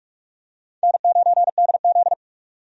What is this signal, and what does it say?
Morse code, heard as a keyed tone, about 46 wpm, 700 Hz. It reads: N9DZ
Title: USA Call Signs - 4 Characters - 45wpm